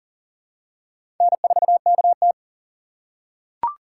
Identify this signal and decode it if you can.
Morse code, heard as a keyed tone, about 40 wpm, 700 Hz. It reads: N4KT
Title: USA Call Signs - 4 Characters - 3x Character Spacing - 40wpm